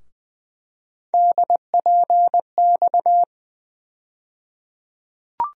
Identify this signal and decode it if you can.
Morse code, heard as a keyed tone, about 20 wpm, 700 Hz. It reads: DPX
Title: USA Call Sign Postfixes - 3 Letters 20wpm